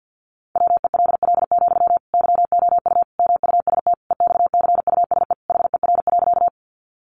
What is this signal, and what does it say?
Morse code, heard as a keyed tone, about 50 wpm, 700 Hz. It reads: WELL, YOU MUST EXCUSE HER.
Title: Sentences from Top 700 Words 50wpm